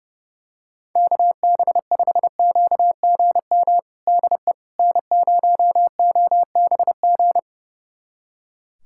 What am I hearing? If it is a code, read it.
Morse code, heard as a keyed tone, about 30 wpm, 700 Hz. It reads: KB5QGM DE N0OBG
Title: USA Callsign DE Callsign 30wpm